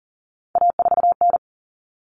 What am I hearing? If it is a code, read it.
Morse code, heard as a keyed tone, about 40 wpm, 700 Hz. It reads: A4N
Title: Common Cut Numbers - 3-Digit 40wpm